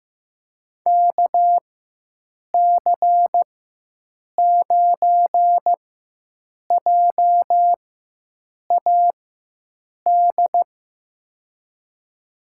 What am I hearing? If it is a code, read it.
Morse code, heard as a keyed tone, about 15 wpm, spaced as for 7 wpm, 700 Hz. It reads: KC9JAD
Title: USA Call Signs - 4x Character Spacing - 15wpm